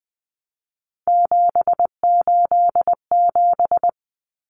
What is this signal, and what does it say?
Morse code, heard as a keyed tone, about 20 wpm, 700 Hz. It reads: 787
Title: Common Cut Numbers - 3-Digit 20wpm